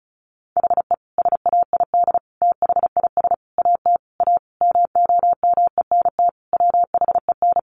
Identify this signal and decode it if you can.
Morse code, heard as a keyed tone, about 34 wpm, 700 Hz. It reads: HE SAID THIS AT A MOMENT WHEN
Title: Sentences from Top 1000 Words 35wpm